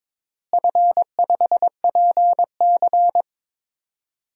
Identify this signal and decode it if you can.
Morse code, heard as a keyed tone, about 22 wpm, 700 Hz. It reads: F5PC
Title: DXCC-100 Call Signs - 4 Characters - Repeated 3x - 22wpm